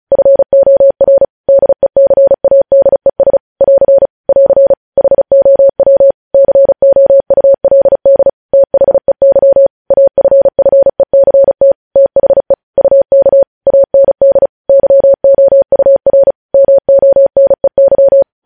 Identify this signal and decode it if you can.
Morse code, heard as a keyed tone, about 36 wpm, 550 Hz. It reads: FOR DECADES <AR> <AR> HOW COULD THEY AFFECT THE UK AND YOUR MONEY
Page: news headlines in morse code at 35 wpm